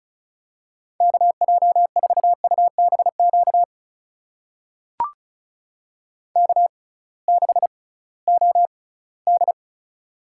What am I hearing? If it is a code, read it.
Morse code, heard as a keyed tone, about 36 wpm, spaced as for 20 wpm, 700 Hz. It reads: KJ4UBQ K6OD
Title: USA Call Signs - 6x Character Spacing - 35wpm